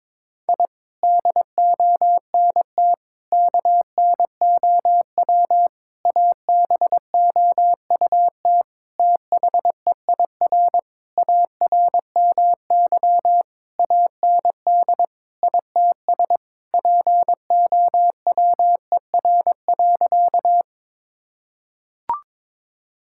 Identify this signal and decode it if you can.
Morse code, heard as a keyed tone, about 21 wpm, 700 Hz. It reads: I DONT KNOW ABOUT THEIR ARMY AND ITS POWER.
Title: Sentences from Top 1000 Words - Review Part 5 - 22wpm